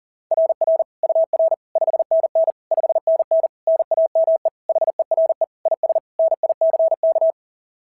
Morse code, heard as a keyed tone, about 37 wpm, 660 Hz. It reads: RR UR 5NN 5NN NAME HERE IS NICK
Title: SOTA QSO Practice 40wpm